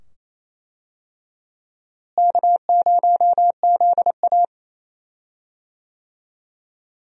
Morse code, heard as a keyed tone, about 27 wpm, 700 Hz. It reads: K0ZA
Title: USA Call Signs - 4 Characters - Repeated 2x - 28wpm